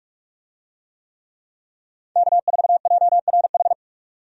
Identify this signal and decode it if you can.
Morse code, heard as a keyed tone, about 46 wpm, 700 Hz. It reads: K4JRH
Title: USA Call Signs - 5 Characters - 45wpm